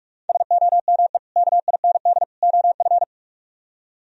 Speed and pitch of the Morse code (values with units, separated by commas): 58 wpm, 700 Hz